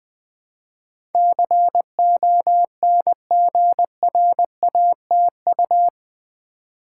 Morse code, heard as a keyed tone, about 20 wpm, 700 Hz. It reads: CONGRATU
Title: Build A Word - 4 Letters At A Time - 12-Letter Words - 20wpm